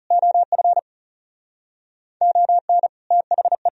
Morse code, heard as a keyed tone, about 38 wpm, 690 Hz. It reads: OF ON THE